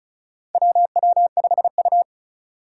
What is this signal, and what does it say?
Morse code, heard as a keyed tone, about 34 wpm, 700 Hz. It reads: WW5U
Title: USA Call Signs - 4 Characters - 2x Character Spacing - 35wpm